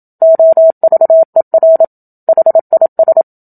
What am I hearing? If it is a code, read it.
Morse code, heard as a keyed tone, about 27 wpm, 650 Hz. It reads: OVER HIS
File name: Book Two / 01-Chapter 1 - 27 WPM